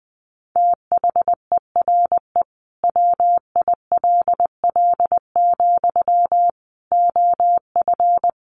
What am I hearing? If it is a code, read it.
Morse code, heard as a keyed tone, about 20 wpm, 700 Hz. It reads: THERE WILL, OF
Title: Sentences from Top 1400 Words 20wpm